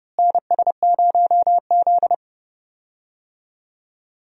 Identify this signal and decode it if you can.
Morse code, heard as a keyed tone, about 32 wpm, 700 Hz. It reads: NS0Z
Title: USA Call Signs - 4 Characters - Repeated 3x - 30wpm